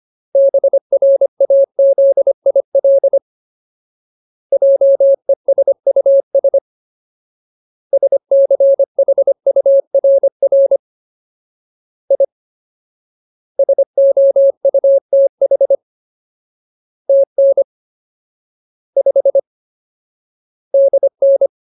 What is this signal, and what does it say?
Morse code, heard as a keyed tone, about 25 wpm, 550 Hz. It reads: BRAZIL JESUS SCHURR I SOUTH TN 5 DN
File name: Ultimate Word List / 25WPM Ultimate